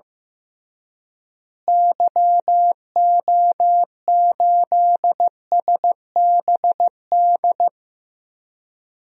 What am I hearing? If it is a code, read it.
Morse code, heard as a keyed tone, about 15 wpm, 700 Hz. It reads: YO8SBD
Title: DXCC-100 Call Signs - 6 Characters - Repeated 3x - 15wpm